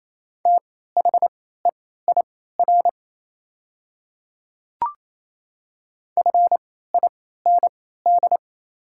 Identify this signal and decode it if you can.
Morse code, heard as a keyed tone, about 28 wpm, spaced as for 16 wpm, 700 Hz. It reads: THEIR FIND
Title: 3x Character Spacing - Top 100 Words - 28wpm